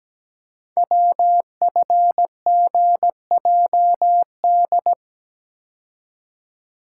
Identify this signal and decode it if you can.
Morse code, heard as a keyed tone, about 17 wpm, 700 Hz. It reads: WFGJD
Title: Code Groups - 5 Letter:Numbers - 17wpm